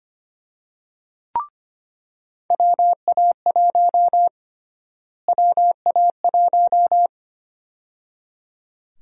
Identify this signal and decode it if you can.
Morse code, heard as a keyed tone, about 25 wpm, 700 Hz. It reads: WA1 WA1
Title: USA Call Sign Prefixes - Repeated 2x - 25wpm